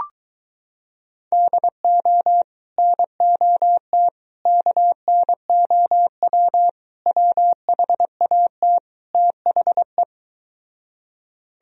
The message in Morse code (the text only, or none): DO NOT KNOW WHAT THE